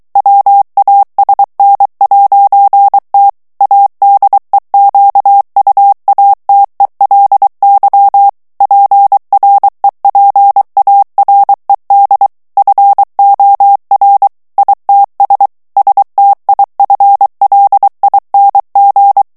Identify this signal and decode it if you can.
Morse code, heard as a keyed tone, about 23 wpm, 800 Hz. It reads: WASN'T ADEQUATELY PREPARED FOR ITS STIFLING
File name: morse mp3 / ANGLAIS texte clair 2 vitesse 22 mots minute